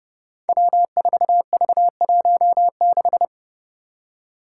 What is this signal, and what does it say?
Morse code, heard as a keyed tone, about 30 wpm, 700 Hz. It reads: W4V16
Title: Five Challenging Letter-Number Combinations 30wpm